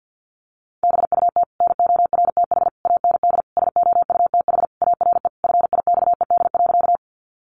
Morse code, heard as a keyed tone, about 49 wpm, 700 Hz. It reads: BUT NORTH AND SOUTH ARE FIXED.
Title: Sentences from Top 1100 Words 50wpm